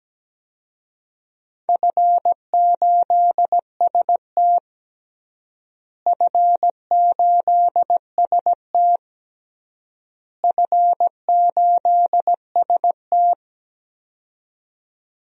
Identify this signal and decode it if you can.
Morse code, heard as a keyed tone, about 17 wpm, 700 Hz. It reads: F8ST F8ST F8ST
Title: DXCC-100 Call Signs - 4 Characters - Repeated 3x - 17wpm